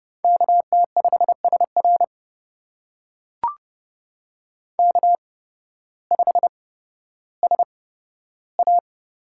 Morse code, heard as a keyed tone, about 29 wpm, spaced as for 14 wpm, 700 Hz. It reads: KT5SR K5SA
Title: USA Call Signs - 5 Characters - 8x Character Spacing - 30wpm